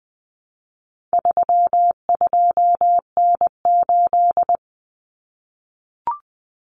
Morse code, heard as a keyed tone, about 20 wpm, 700 Hz. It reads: 32N8
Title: Common Cut Numbers - 4-Digit 20wpm